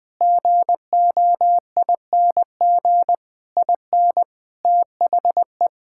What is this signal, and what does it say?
Morse code, heard as a keyed tone, about 21 wpm, 700 Hz. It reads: GOING IN THE
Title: Sets of 8 Words - 20wpm